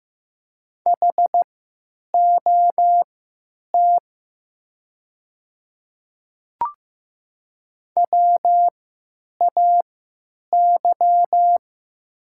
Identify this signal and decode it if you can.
Morse code, heard as a keyed tone, about 15 wpm, spaced as for 9 wpm, 700 Hz. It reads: HOT WAY
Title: 3x Character Spacing - Top 100 Words - 15wpm